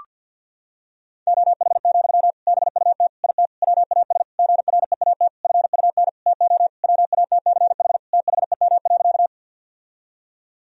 Morse code, heard as a keyed tone, about 52 wpm, 700 Hz. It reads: OH, BUT IT WAS GREAT FUN TO WATCH THEM.
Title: Sentences from Top 1000 Words - Review Part 4 - 50wpm